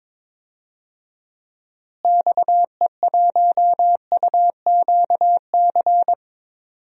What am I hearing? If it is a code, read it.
Morse code, heard as a keyed tone, about 22 wpm, 700 Hz. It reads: XE1UQC
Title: DXCC-100 Call Signs - 6 Characters - Repeated 3x - 22wpm